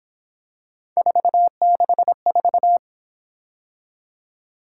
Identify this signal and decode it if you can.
Morse code, heard as a keyed tone, about 26 wpm, 700 Hz. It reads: 464
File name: Speed-Racing - Three Letter-Number Combinations - 22wpm